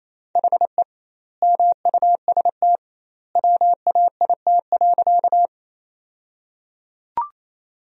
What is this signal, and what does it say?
Morse code, heard as a keyed tone, about 28 wpm, 700 Hz. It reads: HE MUST WAIT.
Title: 2x Word-Spacing - Sentences from Top 500 Words - 28wpm